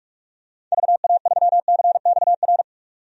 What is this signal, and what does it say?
Morse code, heard as a keyed tone, about 46 wpm, 700 Hz. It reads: VA3CXR
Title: DXCC-100 Call Signs - 6 Characters - 45wpm